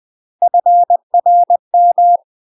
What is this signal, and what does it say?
Morse code, heard as a keyed tone, about 20 wpm, 700 Hz. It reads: FRM
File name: UK MASTER QSOs TRAINING/20WPM